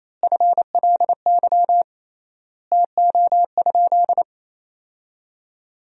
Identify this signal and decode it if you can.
Morse code, heard as a keyed tone, about 28 wpm, 700 Hz. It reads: FLY TO?
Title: 3x Word-Spacing - Sentences from Top 500 Words - 28wpm